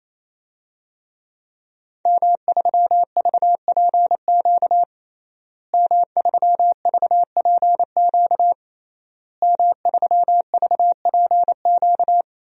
Morse code, heard as a keyed tone, about 28 wpm, 700 Hz. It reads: M3VPQ M3VPQ M3VPQ
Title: DXCC-100 Call Signs - 5 Characters - Repeated 3x - 28wpm